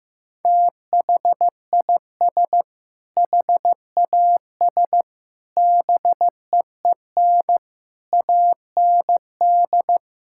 Morse code, heard as a keyed tone, about 16 wpm, 700 Hz. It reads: THIS HAS BEEN AND
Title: Sentences from Top 1600 Words 15wpm